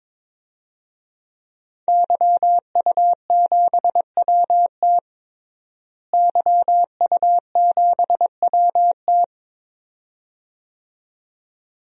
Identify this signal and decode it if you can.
Morse code, heard as a keyed tone, about 21 wpm, 700 Hz. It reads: YU7WT YU7WT
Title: DXCC-100 Call Signs - 5 Characters - Repeated 2x - 22wpm